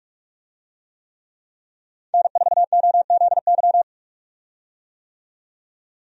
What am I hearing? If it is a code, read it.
Morse code, heard as a keyed tone, about 43 wpm, 700 Hz. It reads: N4OZY
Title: USA Call Signs - 5 Characters - 45wpm